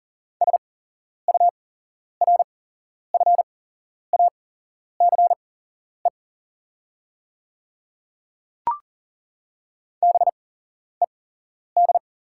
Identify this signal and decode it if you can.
Morse code, heard as a keyed tone, about 40 wpm, spaced as for 11 wpm, 690 Hz. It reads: SURFACE BED
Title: 8x Character Spacing - Top 500 Words - 40wpm